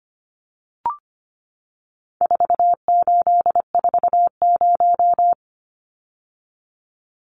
Morse code, heard as a keyed tone, about 25 wpm, 700 Hz. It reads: 4840
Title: Numbers - 4-Digit 25wpm